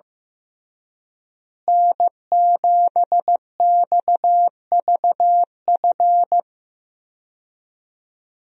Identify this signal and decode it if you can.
Morse code, heard as a keyed tone, about 15 wpm, 700 Hz. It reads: N7XVF
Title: USA Call Signs - 5 Characters - Repeated 2x - 15wpm